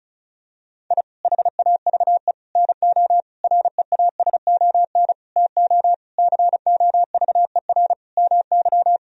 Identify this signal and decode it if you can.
Morse code, heard as a keyed tone, about 36 wpm, 700 Hz. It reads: I HAVE NO REASON TO COVER MY
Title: Sentences from Top 1000 Words - Review Part 1 - 35wpm